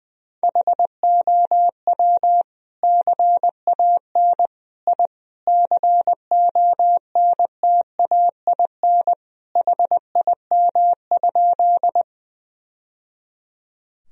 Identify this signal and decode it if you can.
Morse code, heard as a keyed tone, about 21 wpm, 700 Hz. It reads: HOW CAN I CONTAIN HIM?
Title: Sentences from Top 1000 Words - Review Part 2 - 20wpm